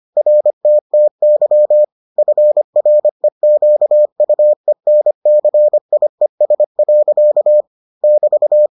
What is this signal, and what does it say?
Morse code, heard as a keyed tone, about 25 wpm, 600 Hz. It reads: RTTY FREQUENCIES. <BT>